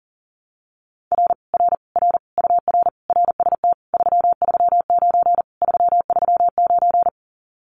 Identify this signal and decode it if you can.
Morse code, heard as a keyed tone, about 40 wpm, 700 Hz. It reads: R R R UR RST 339 339
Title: Common QSO Phrases Encore 40wpm